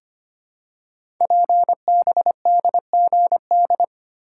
Morse code, heard as a keyed tone, about 25 wpm, 700 Hz. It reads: PBDGD
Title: Code Groups - 5 Letters - 25wpm